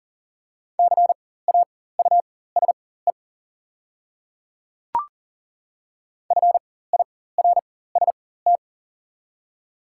Morse code, heard as a keyed tone, about 40 wpm, spaced as for 19 wpm, 690 Hz. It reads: CAUSE FIRST
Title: 4x Character Spacing - Top 200 Words - 40wpm